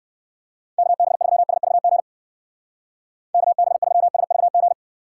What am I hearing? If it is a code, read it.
Morse code, heard as a keyed tone, about 62 wpm, 700 Hz. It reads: KB3SVG KB3SVG